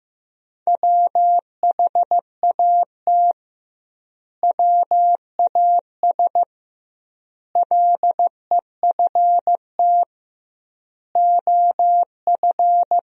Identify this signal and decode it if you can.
Morse code, heard as a keyed tone, about 15 wpm, 700 Hz. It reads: WHAT WAS LEFT OF
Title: 2x Word-Spacing - Sentences from Top 300 Words - 15wpm